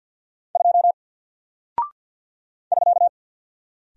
Morse code, heard as a keyed tone, about 48 wpm, 700 Hz. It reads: 2 <SK>
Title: Single Character (Letter:Number:Punctuation:Prosign) Alt 50wpm